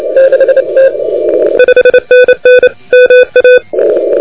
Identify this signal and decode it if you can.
Morse code, heard as a keyed tone, about 27 wpm, 2500 Hz, with noise. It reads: 5NN MA
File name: recordings / YO2LEA de NT1K ARRL10M CW 28MHz